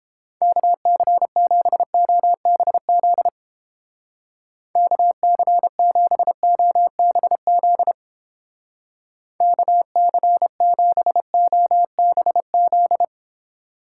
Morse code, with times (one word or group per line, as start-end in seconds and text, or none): 0.4-3.3 KC7OBZ
4.8-7.9 KC7OBZ
9.4-13.0 KC7OBZ